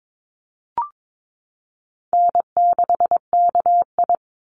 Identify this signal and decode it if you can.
Morse code, heard as a keyed tone, about 22 wpm, 700 Hz. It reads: N6KI
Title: Easy Call Signs - 22wpm